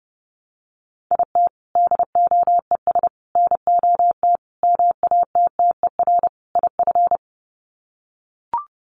Morse code, heard as a keyed tone, about 28 wpm, 700 Hz. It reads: IT DOES NOT MATTER IF
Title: Sets of 5 Words Encore 30wpm